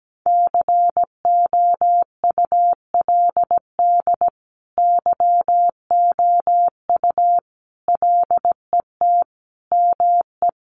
Morse code, heard as a keyed tone, about 17 wpm, 700 Hz. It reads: COULD YOU LET ME
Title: Sentences from Top 1400 Words 17wpm